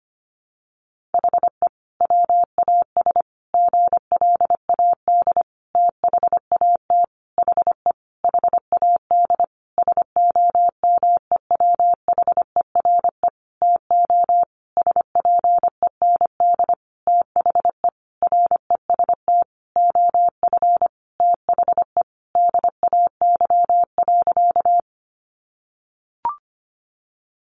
Morse code, heard as a keyed tone, about 25 wpm, 700 Hz. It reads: HE WAS GLAD THAT HE HAD SOMEWHERE TO SPEND THE REST OF THE DAY.
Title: Sentences from Top 1400 Words 25wpm